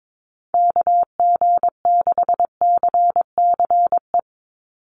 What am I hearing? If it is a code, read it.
Morse code, heard as a keyed tone, about 22 wpm, 700 Hz. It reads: KG6CCE
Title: USA Call Signs 22wpm